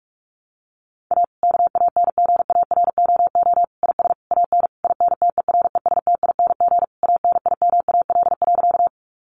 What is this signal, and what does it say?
Morse code, heard as a keyed tone, about 44 wpm, 700 Hz. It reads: A KANGAROO IS AN INTERESTING ANIMAL.
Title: Easy Phrases 45wpm